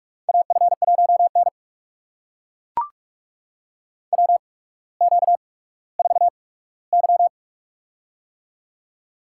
Morse code, heard as a keyed tone, about 42 wpm, spaced as for 19 wpm, 700 Hz. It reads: AF1N WQ4Y
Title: USA Call Signs - 4 Characters - 8x Character Spacing - 45wpm